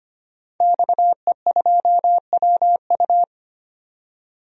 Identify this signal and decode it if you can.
Morse code, heard as a keyed tone, about 25 wpm, 700 Hz. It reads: XE2WU
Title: DXCC-100 Call Signs - 5 Characters - 25wpm